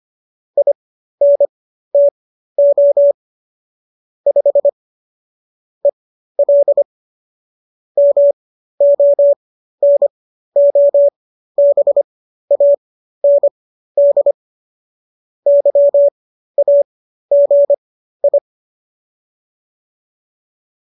Morse code, heard as a keyed tone, about 25 wpm, spaced as for 13 wpm, 570 Hz. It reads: INTO 5 EL MONOBAND YAGI